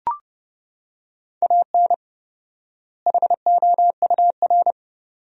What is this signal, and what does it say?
Morse code, heard as a keyed tone, about 32 wpm, 700 Hz. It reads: AN HOUR
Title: 4x Word-Spacing - Sentences from Top 400 Words - 30wpm